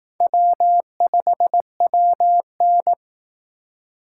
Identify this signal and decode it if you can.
Morse code, heard as a keyed tone, about 18 wpm, 700 Hz. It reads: W5WN
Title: Speed Racing - USA 1x2 Call Signs 15wpm